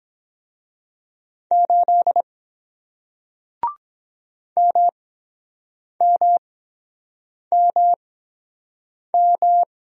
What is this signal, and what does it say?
Morse code, heard as a keyed tone, about 23 wpm, spaced as for 7 wpm, 700 Hz. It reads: 8 MMMM